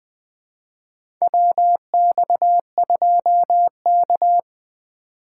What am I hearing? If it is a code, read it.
Morse code, heard as a keyed tone, about 21 wpm, 700 Hz. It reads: WX2K
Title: USA Call Signs - 4 Characters - 3x Character Spacing - 20wpm